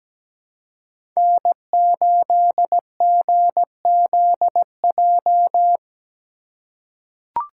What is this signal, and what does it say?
Morse code, heard as a keyed tone, about 17 wpm, 700 Hz. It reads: N8GZJ
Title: USA Call Signs - 5 Characters - 4x Character Spacing - 17wpm